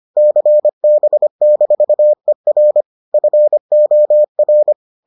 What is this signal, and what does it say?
Morse code, heard as a keyed tone, about 25 wpm, 600 Hz. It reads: CB-ER FOR